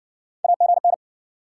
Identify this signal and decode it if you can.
Morse code, heard as a keyed tone, about 60 wpm, 700 Hz. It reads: AGN